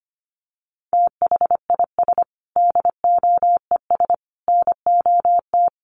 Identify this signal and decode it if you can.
Morse code, heard as a keyed tone, about 25 wpm, 700 Hz. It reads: THIS DOES NOT
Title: Sentences from Top 1400 Words 25wpm